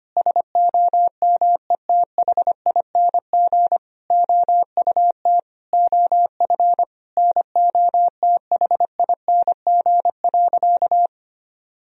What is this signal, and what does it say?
Morse code, heard as a keyed tone, about 25 wpm, 700 Hz. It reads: SOMETHING OUT OF NOTHING.
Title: Sentences from Top 1000 Words - Review Part 3 - 25wpm